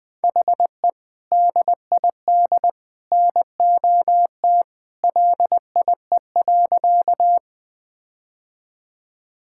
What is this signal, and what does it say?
Morse code, heard as a keyed tone, about 20 wpm, 700 Hz. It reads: HE DID NOT LIE.
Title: Sentences from Top 1600 Words 20wpm